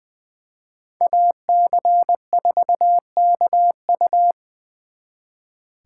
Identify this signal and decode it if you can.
Morse code, heard as a keyed tone, about 24 wpm, 700 Hz. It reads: AC4KU
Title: Speed Racing - USA 2x2 Call Signs 20wpm